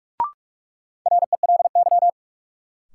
Morse code, heard as a keyed tone, about 46 wpm, 700 Hz. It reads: RELY
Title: Sentences from Top 1200 Words - New Word Drill - 45wpm